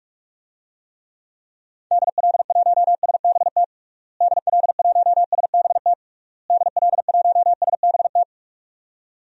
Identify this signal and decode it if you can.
Morse code, heard as a keyed tone, about 46 wpm, 700 Hz. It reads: DL1SBT DL1SBT DL1SBT
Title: DXCC-100 Call Signs - 6 Characters - Repeated 3x - 45wpm